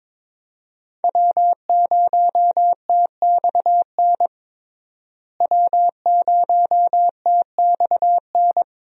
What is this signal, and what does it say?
Morse code, heard as a keyed tone, about 21 wpm, 700 Hz. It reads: W0TXN W0TXN
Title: USA Call Signs - 5 Characters - Repeated 3x - 22wpm